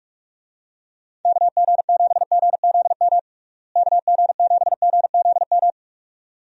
Morse code, heard as a keyed tone, about 46 wpm, 700 Hz. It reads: KG7GZM KG7GZM
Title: USA Call Signs - 6 Characters - Repeated 2x - 45wpm